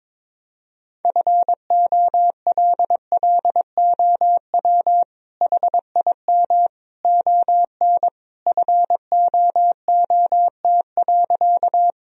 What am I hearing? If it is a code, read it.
Morse code, heard as a keyed tone, about 22 wpm, 700 Hz. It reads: FOLLOW HIM ON FOOT.
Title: Sentences from Top 1000 Words - Review Part 3 - 22wpm